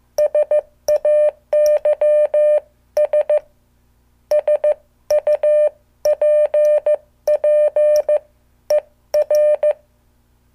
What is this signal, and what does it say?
Morse code, heard as a keyed tone, about 15 wpm, spaced as for 13 wpm, 600 Hz. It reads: SAYS SUPPER